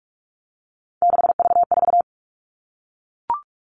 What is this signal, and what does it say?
Morse code, heard as a keyed tone, about 44 wpm, 700 Hz. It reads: 6V4
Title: All Cut Numbers - 3-Digit 45wpm